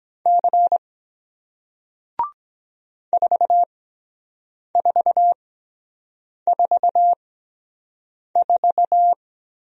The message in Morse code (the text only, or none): C 4444